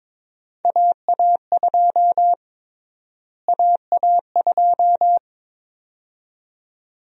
Morse code, heard as a keyed tone, about 22 wpm, 700 Hz. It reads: AA2 AA2
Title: USA Call Sign Prefixes - Repeated 2x - 22wpm